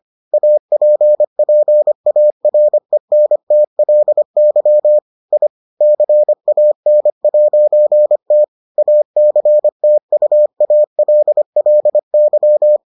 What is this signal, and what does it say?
Morse code, heard as a keyed tone, about 25 wpm, 600 Hz. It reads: APPARENTLY I CAN'T ACTUALLY